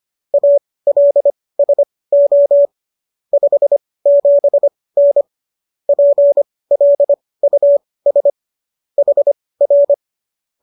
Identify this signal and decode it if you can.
Morse code, heard as a keyed tone, about 25 wpm, spaced as for 18 wpm, 570 Hz. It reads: ALSO 57N PLUS HR
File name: qso218 18